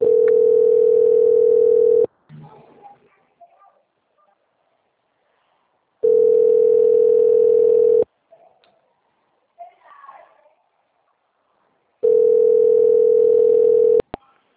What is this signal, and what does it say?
Ringback tone (North America), 440 + 480 Hz.
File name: out- - - - - .40884